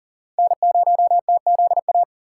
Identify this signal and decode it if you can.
Morse code, heard as a keyed tone, about 54 wpm, spaced as for 47 wpm, 700 Hz. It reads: N0TZA
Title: Speed Racing - USA 1x3 Call Signs 40wpm